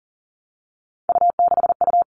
Morse code, heard as a keyed tone, about 40 wpm, 700 Hz. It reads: U6U
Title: All Cut Numbers - 3-Digit 40wpm